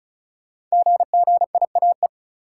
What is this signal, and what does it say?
Morse code, heard as a keyed tone, about 36 wpm, 700 Hz. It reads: GGIAE
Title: Code Groups - 5 Letters - 35wpm